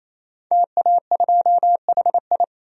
Morse code, heard as a keyed tone, about 28 wpm, 700 Hz. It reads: TA2HI
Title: DXCC-100 Call Signs - 5 Characters - 28wpm